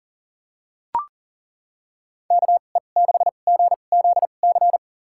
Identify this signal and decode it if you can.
Morse code, heard as a keyed tone, about 40 wpm, spaced as for 29 wpm, 700 Hz. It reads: KE6GZC
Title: USA Call Signs - 6 Characters - 2x Character Spacing - 40wpm